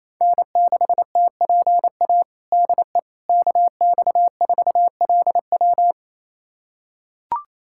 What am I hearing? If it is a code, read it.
Morse code, heard as a keyed tone, about 28 wpm, 700 Hz. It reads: N6TPA DE KX4LW
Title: USA Callsign DE Callsign - 5 Characters - 28wpm